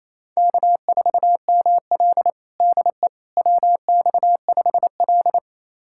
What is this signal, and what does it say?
Morse code, heard as a keyed tone, about 28 wpm, 700 Hz. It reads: K4ML DE WX5L
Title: USA Callsign DE Callsign - 4 Characters - 28wpm